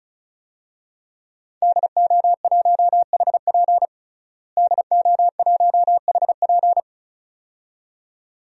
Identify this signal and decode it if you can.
Morse code, heard as a keyed tone, about 36 wpm, 700 Hz. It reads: DO1HP DO1HP
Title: DXCC-100 Call Signs - 5 Characters - Repeated 2x - 35wpm